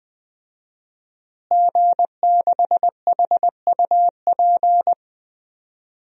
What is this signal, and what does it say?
Morse code, heard as a keyed tone, about 20 wpm, 700 Hz. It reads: G6HUP
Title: DXCC-100 Call Signs - 5 Characters - Repeated 2x - 20wpm